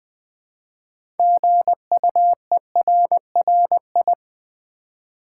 Morse code, heard as a keyed tone, about 20 wpm, 700 Hz. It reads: GUERRI
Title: Build A Word - 3 Letters At A Time - 9-Letter Words - 20wpm